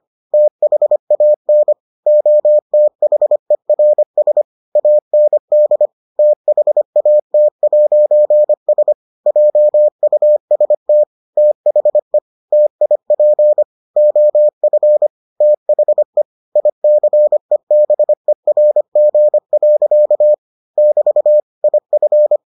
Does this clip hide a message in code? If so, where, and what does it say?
Yes: Morse code from 0.3 to 22.4 s, heard as a keyed tone, about 25 wpm, 600 Hz. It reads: THAN OTHERS AND THAT'S JUST THE TIP OF THE ICEBERG. <BT> IF